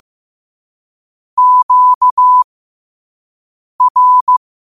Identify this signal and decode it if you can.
Morse code, heard as a keyed tone, about 15 wpm, spaced as for 5 wpm, 1000 Hz. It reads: QR
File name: qotdfiles / 5wpm